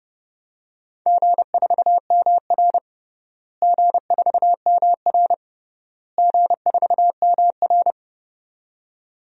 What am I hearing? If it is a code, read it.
Morse code, heard as a keyed tone, about 30 wpm, 700 Hz. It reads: G4MR G4MR G4MR
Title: DXCC-100 Call Signs - 4 Characters - Repeated 3x - 30wpm